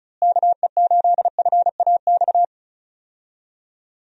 Morse code, heard as a keyed tone, about 34 wpm, 700 Hz. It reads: KE8FAX
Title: USA Call Signs - 6 Characters - Repeated 3x - 35wpm